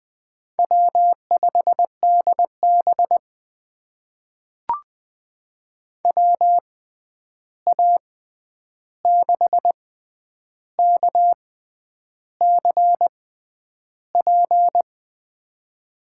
Morse code, heard as a keyed tone, about 20 wpm, spaced as for 9 wpm, 700 Hz. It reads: W5DB WA6KCP
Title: USA Call Signs - 6x Character Spacing - 20wpm